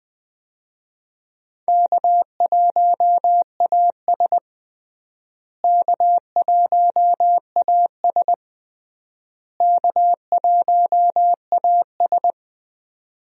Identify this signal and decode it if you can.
Morse code, heard as a keyed tone, about 20 wpm, 700 Hz. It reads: K1AS K1AS K1AS
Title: USA Call Signs - 4 Characters - Repeated 3x - 20wpm